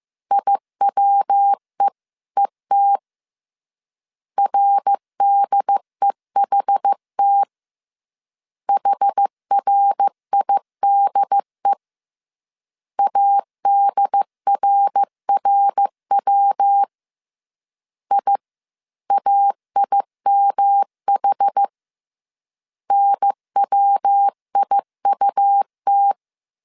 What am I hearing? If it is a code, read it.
Morse code, heard as a keyed tone, about 15 wpm, 800 Hz. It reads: IWEET RDEHT HRIDE ADRRW IAIMH NWIUT